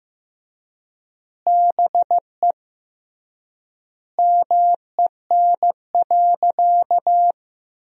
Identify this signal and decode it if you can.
Morse code, heard as a keyed tone, about 16 wpm, 700 Hz. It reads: BE MEN.
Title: 3x Word-Spacing - Sentences from Top 300 Words - 15wpm